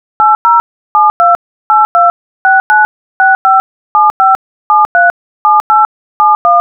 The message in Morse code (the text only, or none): none